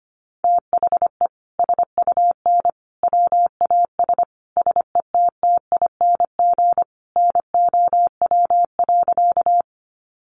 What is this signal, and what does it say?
Morse code, heard as a keyed tone, about 25 wpm, 700 Hz. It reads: THE SUN WAS SETTING NOW.
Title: Sentences from Top 1300 Words 25wpm